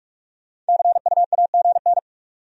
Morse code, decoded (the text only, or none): CUAGN